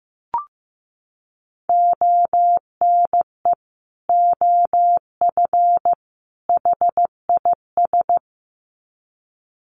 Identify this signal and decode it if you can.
Morse code, heard as a keyed tone, about 15 wpm, 700 Hz. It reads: ONE OF HIS
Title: Sets of 3 Words 15wpm